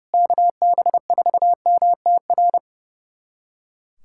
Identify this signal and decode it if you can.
Morse code, heard as a keyed tone, about 30 wpm, 700 Hz. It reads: KB4MTR
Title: USA Callsign DE Callsign 30wpm